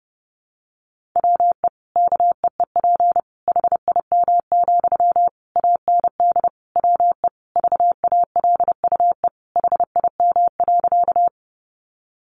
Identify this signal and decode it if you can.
Morse code, heard as a keyed tone, about 30 wpm, 700 Hz. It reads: WE KEEP HIM, AND WE VALUE HIM.
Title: Sentences from Top 600 Words 30wpm